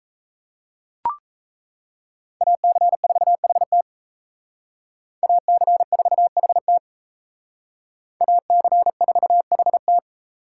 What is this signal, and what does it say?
Morse code, heard as a keyed tone, about 32 wpm, 700 Hz. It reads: AC4HT AC4HT AC4HT
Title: Speed Racing - USA Call Signs 28wpm